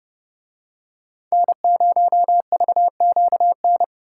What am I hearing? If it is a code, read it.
Morse code, heard as a keyed tone, about 28 wpm, 700 Hz. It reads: N0VQN